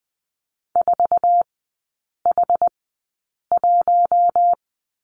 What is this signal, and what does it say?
Morse code, heard as a keyed tone, about 20 wpm, spaced as for 8 wpm, 700 Hz. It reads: 4H1
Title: 060 Lesson 11e - Letter h - Three Characters 20wpm